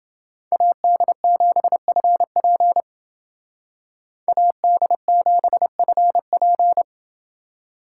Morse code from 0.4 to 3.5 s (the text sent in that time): AD7FP